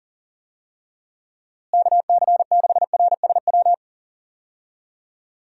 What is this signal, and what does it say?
Morse code, heard as a keyed tone, about 40 wpm, 700 Hz. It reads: KC6RSW
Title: USA Call Signs - 6 Characters - Repeated 3x - 40wpm